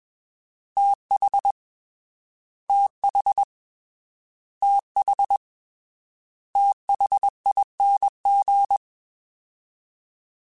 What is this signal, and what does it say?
Morse code, heard as a keyed tone, about 21 wpm, 800 Hz. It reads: TH TH TH THING
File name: TH / TH mp3s / TH Prefixes 3x2 Twice @20